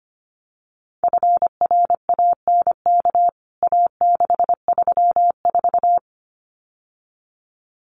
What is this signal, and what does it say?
Morse code, heard as a keyed tone, about 25 wpm, 700 Hz. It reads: FRANK A634
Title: CWT Exchange 25wpm